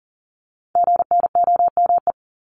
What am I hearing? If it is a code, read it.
Morse code, heard as a keyed tone, about 40 wpm, 700 Hz. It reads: GNOME
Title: Multi-letter Phonemes 40wpm